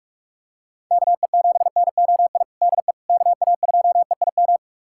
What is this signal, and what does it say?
Morse code, heard as a keyed tone, about 46 wpm, 700 Hz. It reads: KE7NOI DE KA2EIM
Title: USA Callsign DE Callsign - 6 Characters - 45wpm